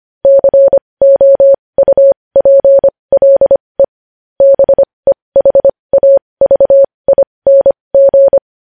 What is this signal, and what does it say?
Morse code, heard as a keyed tone, about 25 wpm, spaced as for 20 wpm, 550 Hz. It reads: COUPLE BEHAVING
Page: news headlines in morse code at 20 wpm